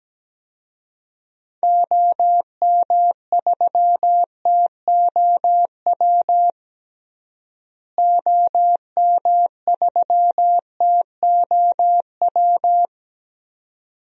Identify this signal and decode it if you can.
Morse code, heard as a keyed tone, about 17 wpm, 700 Hz. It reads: OM3TOW OM3TOW
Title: DXCC-100 Call Signs - 6 Characters - Repeated 2x - 17wpm